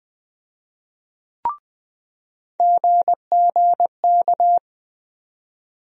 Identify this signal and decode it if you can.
Morse code, heard as a keyed tone, about 20 wpm, 700 Hz. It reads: GGK
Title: USA Call Sign Postfixes - 3 Letters 20wpm